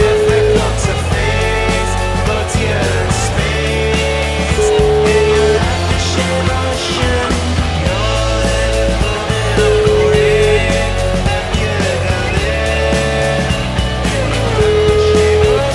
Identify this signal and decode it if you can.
Ringback tone (Europe, 425 Hz), with noise.